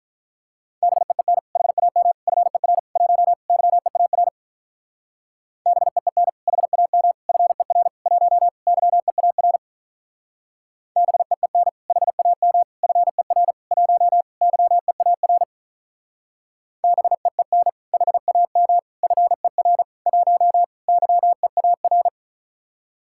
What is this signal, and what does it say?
Morse code, heard as a keyed tone, about 44 wpm, 700 Hz. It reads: BEEN HAM FER 1 YEAR BEEN HAM FER 1 YEAR BEEN HAM FER 1 YEAR BEEN HAM FER 1 YEAR